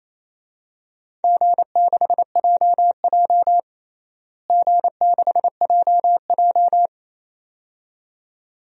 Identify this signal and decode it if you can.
Morse code, heard as a keyed tone, about 28 wpm, 700 Hz. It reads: G6JJ G6JJ
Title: DXCC-100 Call Signs - 4 Characters - Repeated 2x - 28wpm